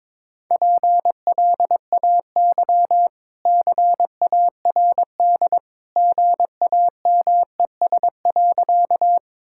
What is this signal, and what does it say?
Morse code, heard as a keyed tone, about 22 wpm, 690 Hz. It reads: PLAY CARD GAMES.
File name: practice / 22wpm / Easy Sentences with Hints - 22wpm